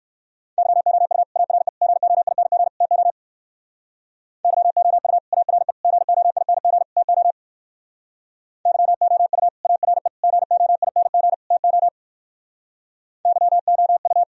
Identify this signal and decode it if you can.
Morse code, heard as a keyed tone, about 62 wpm, 700 Hz. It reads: YOU ARE GOING TO YOU ARE GOING TO YOU ARE GOING TO YOU